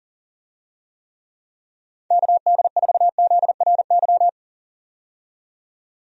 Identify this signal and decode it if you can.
Morse code, heard as a keyed tone, about 40 wpm, 700 Hz. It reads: KD4ZRY
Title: USA Call Signs - 6 Characters - 40wpm